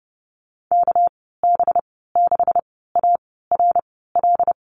Morse code, heard as a keyed tone, about 30 wpm, spaced as for 17 wpm, 700 Hz. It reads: KB6ARL
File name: Lesson 37g - Letter x - Call Signs 30wpm